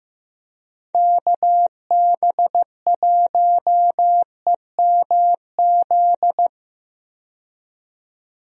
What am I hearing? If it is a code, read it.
Morse code, heard as a keyed tone, about 16 wpm, 700 Hz. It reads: KB1EMZ
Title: USA Call Signs - 6 Characters - 15wpm